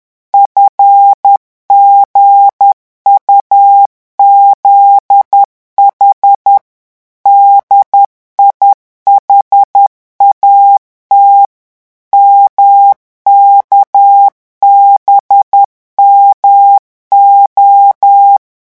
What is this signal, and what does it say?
Morse code, heard as a keyed tone, about 10 wpm, 800 Hz. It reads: FGUZH DIHAT MKBMO
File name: morse mp3 / revision8-vitesse 10 mots